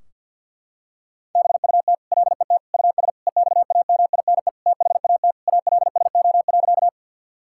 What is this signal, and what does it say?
Morse code, heard as a keyed tone, about 46 wpm, 700 Hz. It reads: BUT LET US EXAMINE THAT ALSO.
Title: Sentences from Top 1600 Words 50wpm